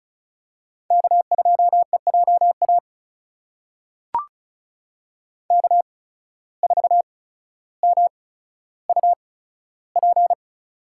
Morse code, heard as a keyed tone, about 36 wpm, spaced as for 15 wpm, 700 Hz. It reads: K2EJA K4MUP
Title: USA Call Signs - 5 Characters - 8x Character Spacing - 35wpm